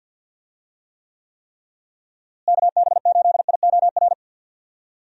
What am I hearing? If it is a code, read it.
Morse code, heard as a keyed tone, about 48 wpm, 700 Hz. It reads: KB8IOR
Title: USA Call Signs - 6 Characters - Repeated 3x - 50wpm